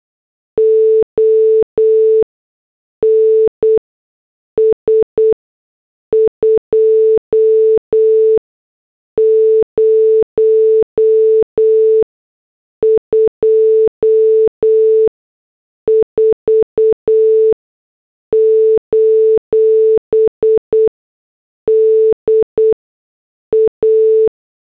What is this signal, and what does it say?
Morse code, heard as a keyed tone, about 8 wpm, 440 Hz. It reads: O N S 2 0 2 4 : D A
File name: morse / headlines